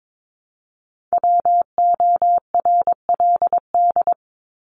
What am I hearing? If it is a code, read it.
Morse code, heard as a keyed tone, about 22 wpm, 700 Hz. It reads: WORLD
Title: Top 200 Words 22wpm